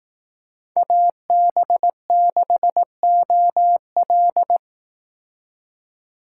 Morse code, heard as a keyed tone, about 18 wpm, 700 Hz. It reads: AB6OL
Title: Speed Racing - USA 2x2 Call Signs 15wpm